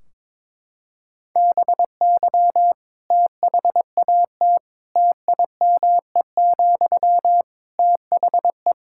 Morse code, heard as a keyed tone, about 21 wpm, 700 Hz. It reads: BY THAT TIME, THE
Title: Sentences from Top 1600 Words 22wpm